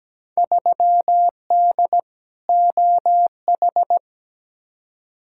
Morse code, heard as a keyed tone, about 17 wpm, 700 Hz. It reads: 3D OH
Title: Field Day Exchanges - 17wpm